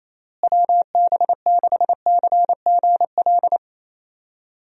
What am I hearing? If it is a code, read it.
Morse code, heard as a keyed tone, about 27 wpm, 700 Hz. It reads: WB6CGL
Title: USA Call Signs - 6 Characters - Repeated 2x - 28wpm